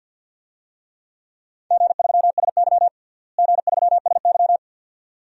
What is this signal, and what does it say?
Morse code, heard as a keyed tone, about 46 wpm, 700 Hz. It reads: G3SY G3SY
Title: DXCC-100 Call Signs - 4 Characters - Repeated 2x - 50wpm